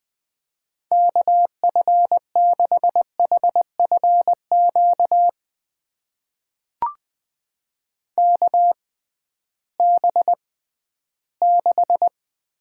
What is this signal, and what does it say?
Morse code, heard as a keyed tone, about 20 wpm, spaced as for 13 wpm, 700 Hz. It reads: KF6HFQ KB6
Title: USA Call Signs - 6 Characters - 6x Character Spacing - 20wpm